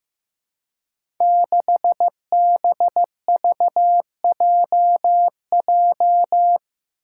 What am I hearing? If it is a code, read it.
Morse code, heard as a keyed tone, about 15 wpm, 700 Hz. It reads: 6BVJJ
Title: Five Challenging Letter-Number Combinations 15wpm